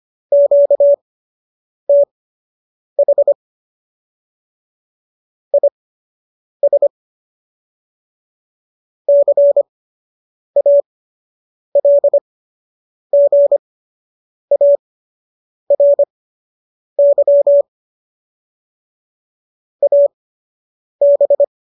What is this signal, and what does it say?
Morse code, heard as a keyed tone, about 25 wpm, spaced as for 8 wpm, 570 Hz. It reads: QTH IS CALGARY AB